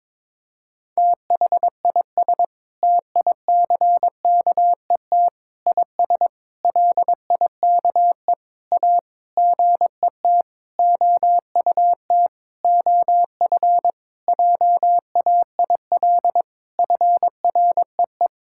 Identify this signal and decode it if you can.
Morse code, heard as a keyed tone, about 21 wpm, 700 Hz. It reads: THIS TICKET IS LIKE A GET OUT OF JAIL FREE
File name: Sentences with 7-Word Phrases - 22wpm